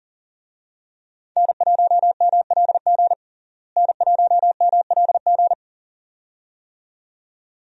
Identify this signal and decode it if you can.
Morse code, heard as a keyed tone, about 40 wpm, 700 Hz. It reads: N1MLG N1MLG
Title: USA Call Signs - 5 Characters - Repeated 2x - 40wpm